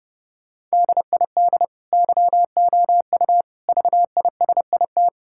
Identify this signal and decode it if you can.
Morse code, heard as a keyed tone, about 30 wpm, 700 Hz. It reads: DID YOU VISIT
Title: Sentences from Top 1000 Words - Review Part 6 - 30wpm